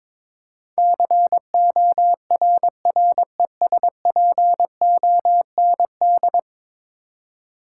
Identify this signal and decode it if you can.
Morse code, heard as a keyed tone, about 22 wpm, 700 Hz. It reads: CORRESPOND
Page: morse-code-ninja-practice-22wpm